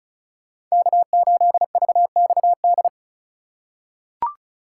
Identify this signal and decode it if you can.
Morse code, heard as a keyed tone, about 34 wpm, 700 Hz. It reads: K8VXD
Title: USA Call Signs - 5 Characters - 8x Character Spacing - 35wpm